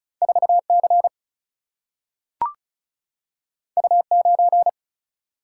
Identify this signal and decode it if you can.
Morse code, heard as a keyed tone, about 36 wpm, 700 Hz. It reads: 4 C U 9
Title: Single-Letter Single-Number 35wpm